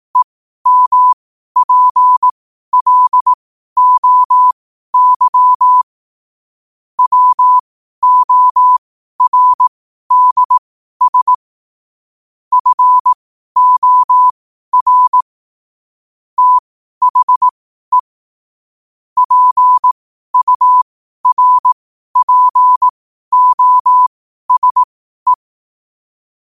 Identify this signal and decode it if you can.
Morse code, heard as a keyed tone, about 18 wpm, spaced as for 12 wpm, 1000 Hz. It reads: EMPLOY WORDS FOR THE PURPOSE
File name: qotdfiles / 12wpm